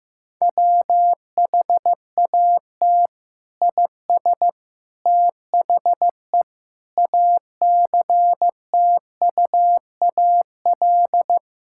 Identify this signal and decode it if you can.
Morse code, heard as a keyed tone, about 15 wpm, 700 Hz. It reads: WHAT IS THE ACTUAL
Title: Sentences from Top 1000 Words - Review Part 5 - 15wpm